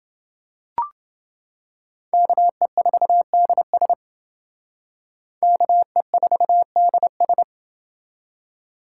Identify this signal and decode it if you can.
Morse code, heard as a keyed tone, about 28 wpm, 700 Hz. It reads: KE4DS KE4DS